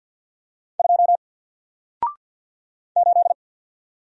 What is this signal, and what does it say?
Morse code, heard as a keyed tone, about 48 wpm, 700 Hz. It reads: 2 8
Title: Single Letter-Number Alt 50wpm